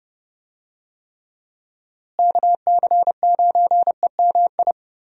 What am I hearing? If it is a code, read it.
Morse code, heard as a keyed tone, about 30 wpm, 700 Hz. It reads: KC9EMI
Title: USA Call Signs - 6 Characters - Repeated 2x - 30wpm